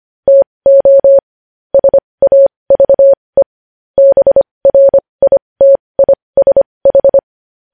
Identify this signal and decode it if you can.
Morse code, heard as a keyed tone, about 25 wpm, spaced as for 20 wpm, 550 Hz. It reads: TO SAVE BRITISH
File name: podcast / News headlines-1x-25x20